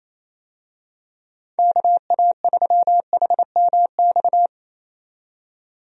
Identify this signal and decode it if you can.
Morse code, heard as a keyed tone, about 27 wpm, 700 Hz. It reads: KA3HMX
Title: USA Call Signs - 6 Characters - 28wpm